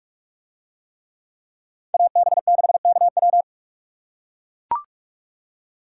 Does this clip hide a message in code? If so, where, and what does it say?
Yes: Morse code from 1.9 to 3.4 s, heard as a keyed tone, about 44 wpm, 700 Hz. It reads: AB6KW